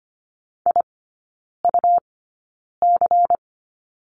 Morse code, heard as a keyed tone, about 25 wpm, spaced as for 9 wpm, 700 Hz. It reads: IUC
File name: Lesson 17e - Letter u - Three Characters 25wpm